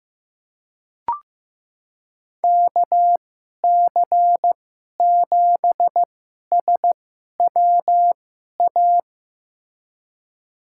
Morse code, heard as a keyed tone, about 16 wpm, spaced as for 11 wpm, 700 Hz. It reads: KC7SWA
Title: USA Call Signs - 6 Characters - 2x Character Spacing - 15wpm